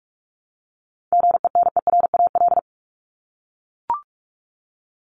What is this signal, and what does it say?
Morse code, heard as a keyed tone, about 44 wpm, 700 Hz. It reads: GENERAL
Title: LY Suffix - Top 100 Words with Base Words 45wpm